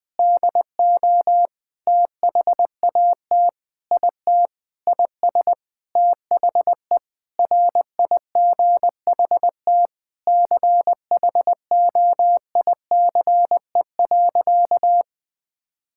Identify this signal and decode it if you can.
Morse code, heard as a keyed tone, about 21 wpm, 700 Hz. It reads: DO THAT IT IS THE RIGHT CHOICE.
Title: Sentences with 8-Word Phrases - Spoken Beginning - 20wpm